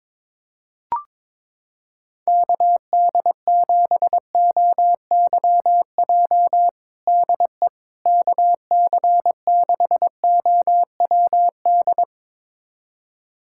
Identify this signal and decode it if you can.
Morse code, heard as a keyed tone, about 21 wpm, 700 Hz. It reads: KD7OYJ DE KC6OWD
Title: USA Callsign DE Callsign - 6 Characters - 22wpm